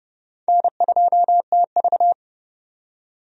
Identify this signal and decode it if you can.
Morse code, heard as a keyed tone, about 30 wpm, 700 Hz. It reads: N2TV
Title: USA Call Signs - 4 Characters - 30wpm